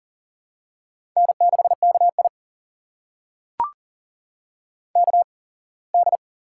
Morse code, heard as a keyed tone, about 39 wpm, spaced as for 24 wpm, 700 Hz. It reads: N6KI KD
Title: USA Call Signs - 8x Character Spacing - 40wpm